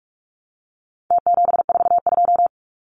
Morse code, heard as a keyed tone, about 44 wpm, 700 Hz. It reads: T742
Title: Numbers - Contest Style - 4-Digit 45wpm